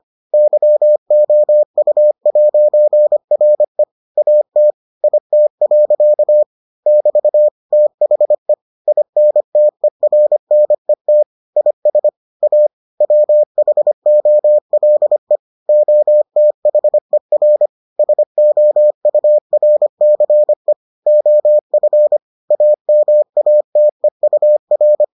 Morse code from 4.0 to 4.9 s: AT

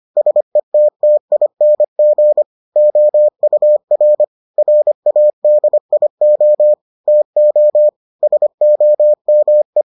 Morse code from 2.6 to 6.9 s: OUR RADIO

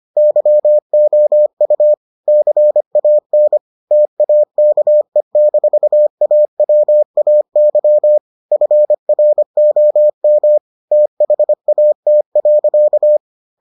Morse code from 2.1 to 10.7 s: CAN TAKE-AWAY FROM

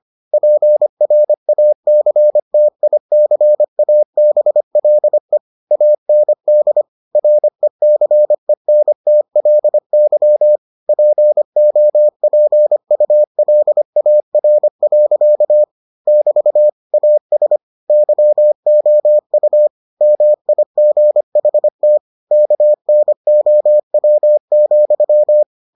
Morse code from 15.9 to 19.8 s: <BT> AS YOU

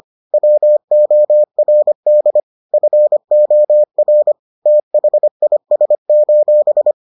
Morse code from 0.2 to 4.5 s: WORD FOR

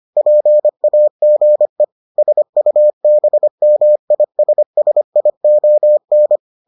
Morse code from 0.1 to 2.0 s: PAGE

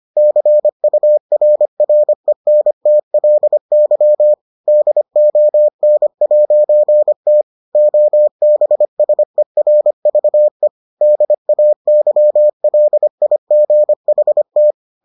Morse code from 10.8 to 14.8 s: DAYLIGHT

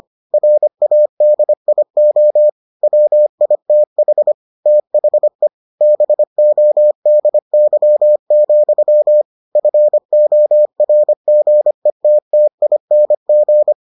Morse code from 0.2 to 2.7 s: RADIO

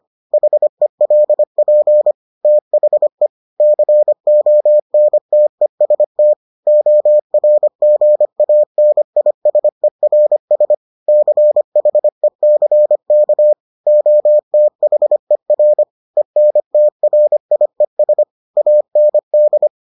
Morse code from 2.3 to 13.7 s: THE CONTEST ORGANISERS CHECK

